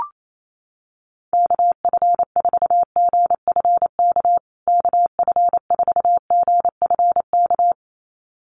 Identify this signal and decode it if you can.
Morse code, heard as a keyed tone, about 28 wpm, 700 Hz. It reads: KF4GFK KF4GFK
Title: USA Call Signs Repeated 2x 28wpm